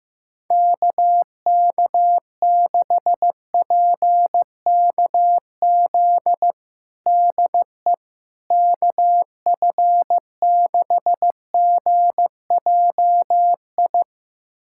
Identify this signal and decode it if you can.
Morse code, heard as a keyed tone, about 15 wpm, 700 Hz. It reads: KK6PKZ DE KF6GJI
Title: USA Callsign DE Callsign 15wpm